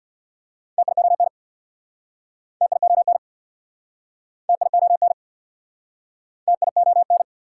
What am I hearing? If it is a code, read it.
Morse code, heard as a keyed tone, about 69 wpm, 690 Hz. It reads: TION TION TION TION